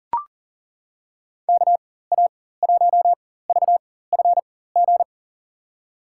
Morse code, heard as a keyed tone, about 40 wpm, spaced as for 19 wpm, 700 Hz. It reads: KA1VFG
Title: USA Call Signs - 6 Characters - 4x Character Spacing - 40wpm